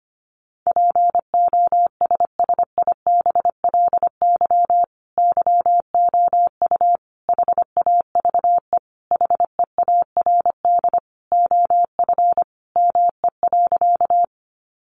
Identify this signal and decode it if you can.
Morse code, heard as a keyed tone, about 25 wpm, 700 Hz. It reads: POSSIBLY YOU HAVE HEARD OF ME.
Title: Sentences from Top 1300 Words 25wpm